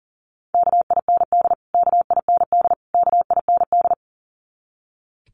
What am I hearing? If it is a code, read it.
Morse code, heard as a keyed tone, about 40 wpm, 700 Hz. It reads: KIND KIND KIND
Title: Top 300 Words Repeated 40wpm